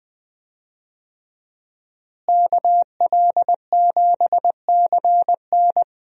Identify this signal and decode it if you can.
Morse code, heard as a keyed tone, about 20 wpm, 700 Hz. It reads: KL7CN
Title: USA Call Signs - 5 Characters - Repeated 2x - 20wpm